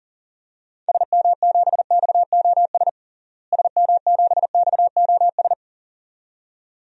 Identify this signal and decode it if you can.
Morse code, heard as a keyed tone, about 40 wpm, 700 Hz. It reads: SM7XOS SM7XOS
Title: DXCC-100 Call Signs - 6 Characters - Repeated 2x - 40wpm